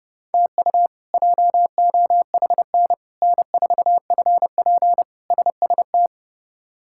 Morse code, heard as a keyed tone, about 32 wpm, 700 Hz. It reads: TU JOHN N4FP SST
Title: K1USN SST QSO Practice - 30wpm